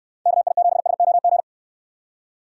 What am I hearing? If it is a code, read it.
Morse code, heard as a keyed tone, about 66 wpm, 700 Hz. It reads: KI7SOG